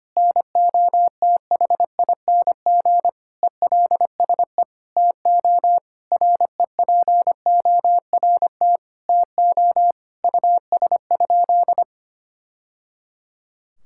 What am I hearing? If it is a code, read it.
Morse code, heard as a keyed tone, about 25 wpm, 700 Hz. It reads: NOTHING ELSE TO REPORT TO US?
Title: Sentences from Top 1000 Words - Review Part 6 - 25wpm